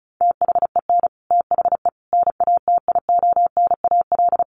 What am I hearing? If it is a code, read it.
Morse code, heard as a keyed tone, about 34 wpm, 700 Hz. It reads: THEN THE NATIONAL
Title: Sentences from Top 1500 Words 35wpm